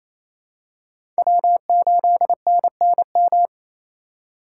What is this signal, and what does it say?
Morse code, heard as a keyed tone, about 28 wpm, 700 Hz. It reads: W8NNM
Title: USA Call Signs - 5 Characters - 4x Character Spacing - 28wpm